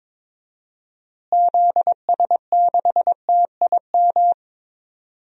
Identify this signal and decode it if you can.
Morse code, heard as a keyed tone, about 21 wpm, 700 Hz. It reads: ZS6TIM
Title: DXCC-100 Call Signs - 6 Characters - Repeated 2x - 22wpm